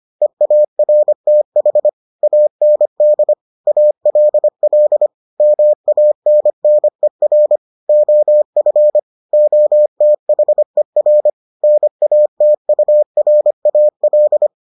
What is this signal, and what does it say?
Morse code, heard as a keyed tone, about 25 wpm, 600 Hz. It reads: EARTH AND ALL MANNER OF OTHER NATURAL